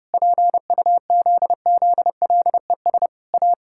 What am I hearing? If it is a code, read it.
Morse code, heard as a keyed tone, about 30 wpm, 700 Hz. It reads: PUZZLES A
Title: Pangram Challenge v2 - 30wpm